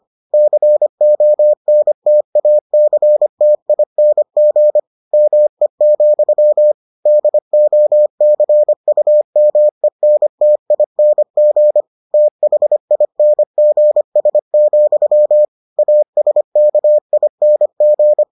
Morse code, heard as a keyed tone, about 25 wpm, 600 Hz. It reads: CONTACTING ME, DOCUMENTING THINGS, ASKING